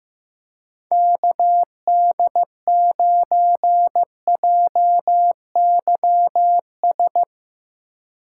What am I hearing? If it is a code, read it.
Morse code, heard as a keyed tone, about 15 wpm, 700 Hz. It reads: KD9JYS
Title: USA Call Signs - 6 Characters - 4x Character Spacing - 15wpm